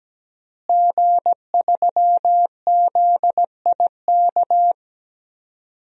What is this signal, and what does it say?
Morse code, heard as a keyed tone, about 17 wpm, 700 Hz. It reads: G3ZIK
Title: DXCC-100 Call Signs - 5 Characters - Repeated 2x - 17wpm